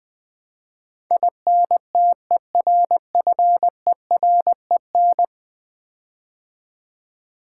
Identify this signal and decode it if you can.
Morse code, heard as a keyed tone, about 20 wpm, 700 Hz. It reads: INTERFEREN
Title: Build A Word - 5 Letters At A Time - 12-Letter Words - 20wpm